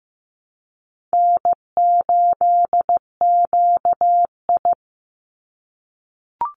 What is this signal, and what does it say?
Morse code, heard as a keyed tone, about 15 wpm, 700 Hz. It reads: N8QI
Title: USA Call Signs 15wpm